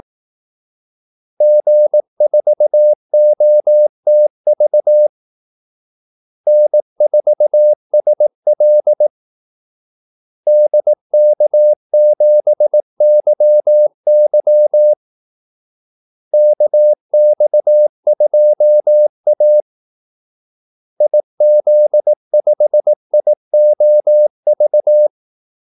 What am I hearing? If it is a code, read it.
Morse code, heard as a keyed tone, about 18 wpm, 600 Hz. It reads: G4OTV N4SL DK7YY KX2A IZ5IOV
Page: Kaldesignaler 18 wpm